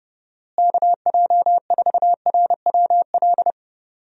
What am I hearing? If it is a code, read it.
Morse code, heard as a keyed tone, about 30 wpm, 700 Hz. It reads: KJ4RWL
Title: USA Call Signs - 6 Characters - Repeated 3x - 30wpm